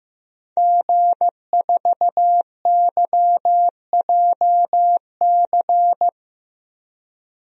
Morse code, heard as a keyed tone, about 15 wpm, 700 Hz. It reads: G4YJC
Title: DXCC-100 Call Signs - 5 Characters - 15wpm